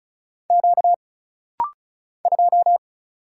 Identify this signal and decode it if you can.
Morse code, heard as a keyed tone, about 36 wpm, 700 Hz. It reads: Q 2
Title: Single Letter-Number - Rapid-Fire 35wpm